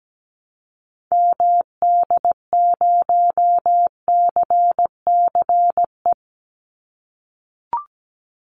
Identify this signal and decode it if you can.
Morse code, heard as a keyed tone, about 17 wpm, 700 Hz. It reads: MD0CCE
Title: Call Signs - 17wpm